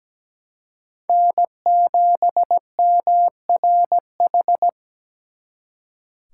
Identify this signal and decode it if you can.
Morse code, heard as a keyed tone, about 20 wpm, 700 Hz. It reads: N7MRH
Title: Speed Racing - USA 1x3 Call Signs 17wpm